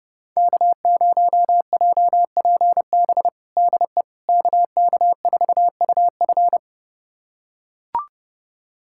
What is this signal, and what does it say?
Morse code, heard as a keyed tone, about 30 wpm, 700 Hz. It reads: K0JPB DE KK4UF
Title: USA Callsign DE Callsign - 5 Characters - 30wpm